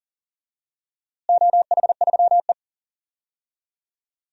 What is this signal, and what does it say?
Morse code, heard as a keyed tone, about 40 wpm, 700 Hz. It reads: OH3E
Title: DXCC-100 Call Signs - 4 Characters - Repeated 3x - 40wpm